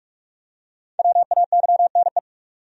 Morse code, heard as a keyed tone, about 44 wpm, 700 Hz. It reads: WAYNE
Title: USA Common Names - Max 5 Letters - 45wpm